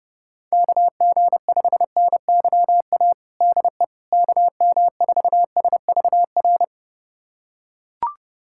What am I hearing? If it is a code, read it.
Morse code, heard as a keyed tone, about 32 wpm, 700 Hz. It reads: KG5NYA DE KM4SVR
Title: USA Callsign DE Callsign - 6 Characters - 30wpm